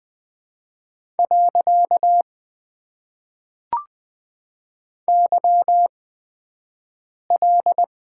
Morse code, heard as a keyed tone, about 20 wpm, spaced as for 5 wpm, 700 Hz. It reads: . YL
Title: Top 90 QSO Elements - 8x Character Spacing - 20wpm